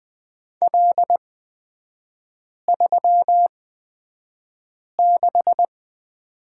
Morse code, heard as a keyed tone, about 20 wpm, spaced as for 5 wpm, 690 Hz. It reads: L36
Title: 236 Lesson 38e - Letter q - Three Characters 20wpm